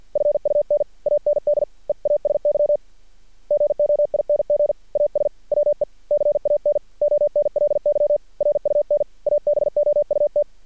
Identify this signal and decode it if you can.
Morse code, heard as a keyed tone, about 50 wpm, 590 Hz. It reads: FUN AND EASY GOING AS WE CAN ONLY RUN ABOUT
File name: archived / CRSnet